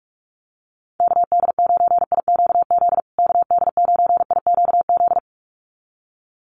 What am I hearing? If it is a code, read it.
Morse code, heard as a keyed tone, about 44 wpm, 700 Hz. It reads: KD9IQZ KD9IQZ
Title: USA Call Signs Repeated 2x 45wpm